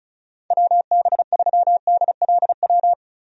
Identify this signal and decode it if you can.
Morse code, heard as a keyed tone, about 34 wpm, 700 Hz. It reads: WB3DLW
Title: USA Call Signs - 6 Characters - 35wpm